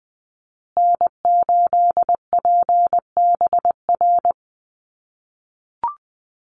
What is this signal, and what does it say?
Morse code, heard as a keyed tone, about 20 wpm, 700 Hz. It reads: N8PBR
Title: USA Call Signs Repeated 2x 20wpm